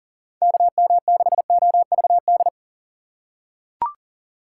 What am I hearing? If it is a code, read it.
Morse code, heard as a keyed tone, about 37 wpm, 700 Hz. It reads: KM6OVD
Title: USA Call Signs - 4x Character Spacing - 40wpm